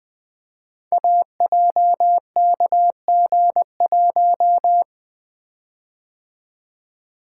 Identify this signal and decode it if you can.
Morse code, heard as a keyed tone, about 20 wpm, 700 Hz. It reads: AJKG1
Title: Code Groups - 5 Letter:Numbers - 20wpm